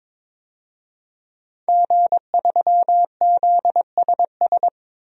Morse code, heard as a keyed tone, about 23 wpm, 700 Hz. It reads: G3ZSS
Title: DXCC-100 Call Signs - 5 Characters - 22wpm